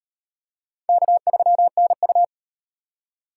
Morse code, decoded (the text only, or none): K3NU